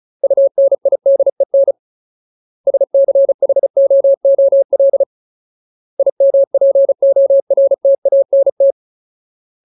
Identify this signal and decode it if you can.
Morse code, heard as a keyed tone, about 34 wpm, 550 Hz. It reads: UNIDEN SCHOOL IMPORTANT
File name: Ultimate Word List / 35WPM Ultimate